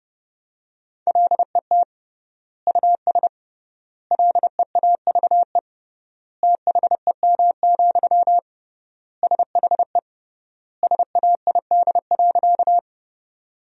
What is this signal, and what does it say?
Morse code, heard as a keyed tone, about 30 wpm, 700 Hz. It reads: LET US LEAVE THEM, SHE SAID.
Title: 3x Word-Spacing - Sentences from Top 400 Words - 30wpm